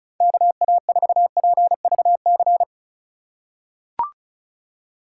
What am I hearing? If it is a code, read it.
Morse code, heard as a keyed tone, about 36 wpm, 700 Hz. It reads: KA4PVC
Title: USA Call Signs - 6 Characters - 3x Character Spacing - 35wpm